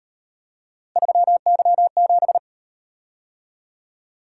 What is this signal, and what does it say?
Morse code, heard as a keyed tone, about 38 wpm, 700 Hz. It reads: 3Y7